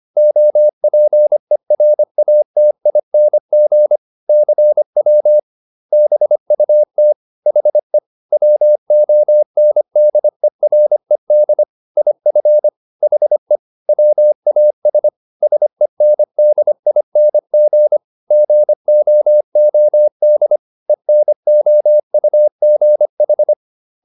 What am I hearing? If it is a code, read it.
Morse code, heard as a keyed tone, about 25 wpm, 600 Hz. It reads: OPERATING CW BUT HE WONDERED IF HE WAS SENDING GOOD ENOUGH